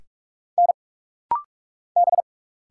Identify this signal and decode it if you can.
Morse code, heard as a keyed tone, about 45 wpm, 690 Hz. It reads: N B
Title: Single Letters - Rapid-Fire 45wpm